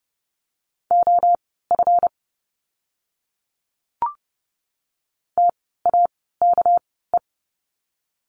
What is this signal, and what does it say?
Morse code, heard as a keyed tone, about 30 wpm, spaced as for 17 wpm, 700 Hz. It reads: OF TAKE
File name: course / 30:10wpm / 237 Lesson 38f - Letter q - Words 30wpm